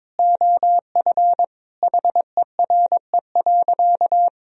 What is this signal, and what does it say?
Morse code, heard as a keyed tone, about 22 wpm, 700 Hz. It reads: OF HERE.
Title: Sentences from Top 1000 Words - Review Part 1 - 22wpm